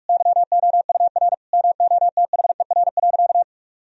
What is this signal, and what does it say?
Morse code, heard as a keyed tone, about 41 wpm, 700 Hz. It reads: YOUR MOTHER.
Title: Sentences from Top 1000 Words - Review Part 3 - 45wpm